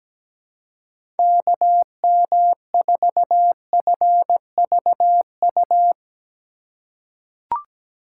Morse code, heard as a keyed tone, about 17 wpm, 700 Hz. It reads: KM4FVU
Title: USA Call Signs - 4x Character Spacing - 17wpm